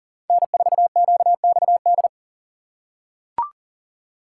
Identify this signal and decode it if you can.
Morse code, heard as a keyed tone, about 40 wpm, 700 Hz. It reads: N4QXD
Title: USA Call Signs - 5 Characters - 4x Character Spacing - 40wpm